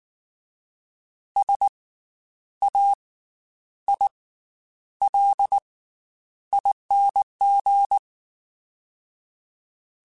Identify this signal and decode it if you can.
Morse code, heard as a keyed tone, about 19 wpm, spaced as for 9 wpm, 800 Hz. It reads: SAILING
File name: Special INGs / Special INGs-mp3s / Slow ing @18